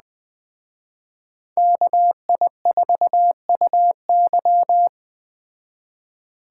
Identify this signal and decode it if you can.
Morse code, heard as a keyed tone, about 20 wpm, 700 Hz. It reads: KI4UY